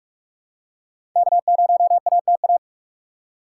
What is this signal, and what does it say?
Morse code, heard as a keyed tone, about 42 wpm, 700 Hz. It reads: K0ATA
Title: USA Call Signs - 6x Character Spacing - 45wpm